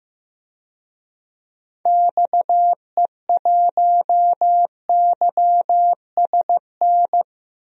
Morse code, heard as a keyed tone, about 15 wpm, 700 Hz. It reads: XE1YSN
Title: DXCC-100 Call Signs - 6 Characters - 15wpm